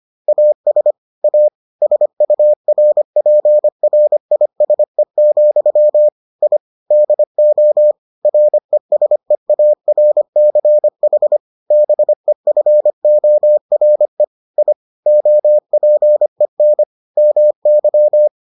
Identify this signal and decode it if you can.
Morse code, heard as a keyed tone, about 25 wpm, 600 Hz. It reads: AS A SURPRISE, I DO RESEARCH BEFORE I OPEN MY